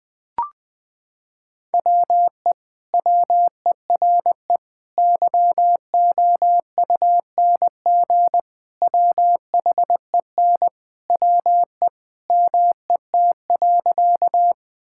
Morse code, heard as a keyed tone, about 21 wpm, 700 Hz. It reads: WE WERE YOUNG WHEN WE MET.
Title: Sentences with 2-Word Phrases - 20wpm